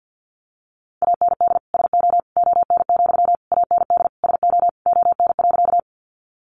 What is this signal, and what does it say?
Morse code, heard as a keyed tone, about 50 wpm, 700 Hz. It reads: AND SO ON, AND SO ON.
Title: Sentences from Top 100 Words 50wpm